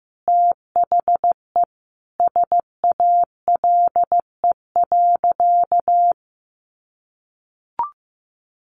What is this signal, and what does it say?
Morse code, heard as a keyed tone, about 15 wpm, 700 Hz. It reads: THE SALE.
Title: Sentences from Top 1400 Words 15wpm